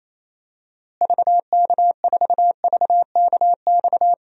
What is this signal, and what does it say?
Morse code, heard as a keyed tone, about 28 wpm, 700 Hz. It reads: VK4VKX
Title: DXCC-100 Call Signs - 6 Characters - Repeated 2x - 28wpm